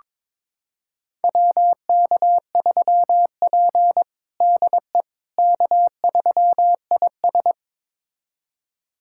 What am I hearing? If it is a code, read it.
Morse code, heard as a keyed tone, about 22 wpm, 700 Hz. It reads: WK3P DE K3IS
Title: USA Callsign DE Callsign - 4 Characters - 22wpm